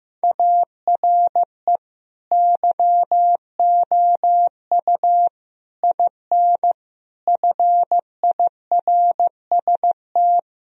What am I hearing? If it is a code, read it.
Morse code, heard as a keyed tone, about 15 wpm, 700 Hz. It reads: ARE YOU IN FIRST
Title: Sentences from Top 1000 Words - Review Part 2 - 15wpm